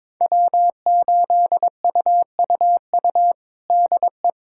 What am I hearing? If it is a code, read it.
Morse code, heard as a keyed tone, about 22 wpm, 700 Hz. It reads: W8UUU DE
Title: USA Callsign DE Callsign - 5 Characters - 22wpm